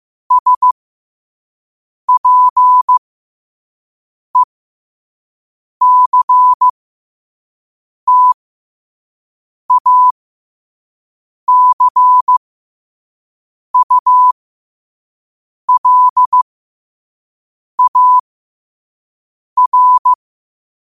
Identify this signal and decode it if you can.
Morse code, heard as a keyed tone, about 15 wpm, spaced as for 5 wpm, 1000 Hz. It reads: SPECTACULAR
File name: qotdfiles / 5wpm